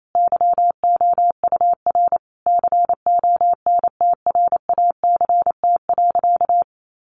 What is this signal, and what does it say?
Morse code, heard as a keyed tone, about 28 wpm, 700 Hz. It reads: YOUR CONTRACT.
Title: Sentences from Top 1400 Words 28wpm